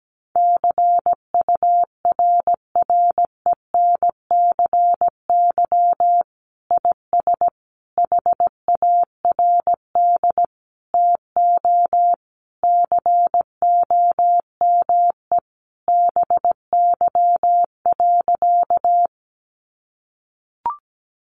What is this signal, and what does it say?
Morse code, heard as a keyed tone, about 17 wpm, 700 Hz. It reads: CURRENCY IS HARD TO COME BY.
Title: Sentences from Top 1500 Words 17wpm